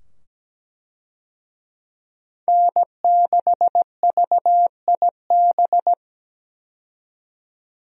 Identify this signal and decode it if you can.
Morse code, heard as a keyed tone, about 17 wpm, 700 Hz. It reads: N6VIB
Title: USA Call Signs - 5 Characters - 17wpm